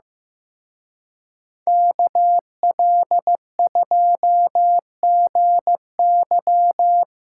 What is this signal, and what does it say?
Morse code, heard as a keyed tone, about 18 wpm, 700 Hz. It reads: KL2GY
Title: Speed Racing - USA 2x2 Call Signs 15wpm